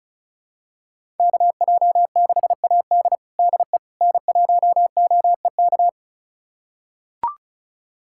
Morse code, heard as a keyed tone, about 34 wpm, 700 Hz. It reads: KJ6AD DE N1OEK
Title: USA Callsign DE Callsign - 5 Characters - 35wpm